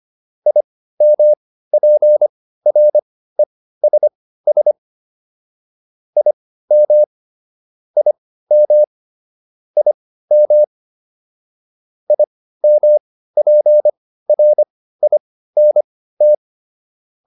Morse code, heard as a keyed tone, about 25 wpm, spaced as for 15 wpm, 600 Hz. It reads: IMPRESS IM IM IM IMPRINT